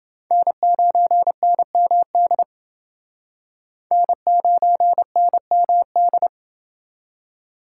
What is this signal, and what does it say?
Morse code, heard as a keyed tone, about 28 wpm, 700 Hz. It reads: N9NMD N9NMD